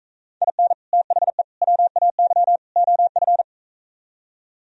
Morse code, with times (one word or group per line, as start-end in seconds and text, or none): 0.4-0.7 IN
0.9-1.4 THE
1.6-2.6 WAY
2.8-3.4 OF